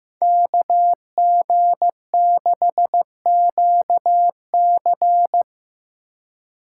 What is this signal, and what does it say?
Morse code, heard as a keyed tone, about 18 wpm, 700 Hz. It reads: KG6QC
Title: Speed Racing - USA 2x2 Call Signs 15wpm